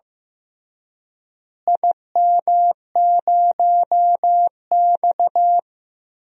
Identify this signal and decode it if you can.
Morse code, heard as a keyed tone, about 15 wpm, 700 Hz. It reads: IM0X
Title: DXCC-100 Call Signs - 4 Characters - Repeated 2x - 15wpm